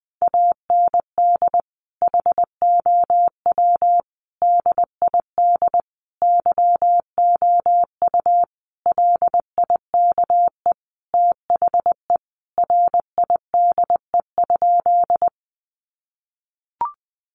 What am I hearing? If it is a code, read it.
Morse code, heard as a keyed tone, about 20 wpm, 700 Hz. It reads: AND HOW DID YOU LIKE THE RIDE?
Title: Sentences from Top 1200 Words 20wpm